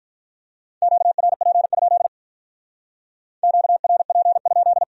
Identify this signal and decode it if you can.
Morse code, heard as a keyed tone, about 45 wpm, 700 Hz. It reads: QRP? QRP?